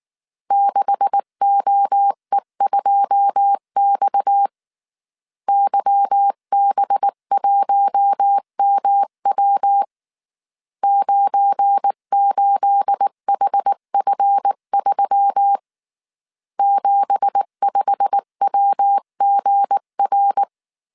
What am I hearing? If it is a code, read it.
Morse code, heard as a keyed tone, about 19 wpm, 800 Hz. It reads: 6OE2X YB1MW 98HF3 75WGR
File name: CW / DET / 1 9 1500C S C 18wpm 2539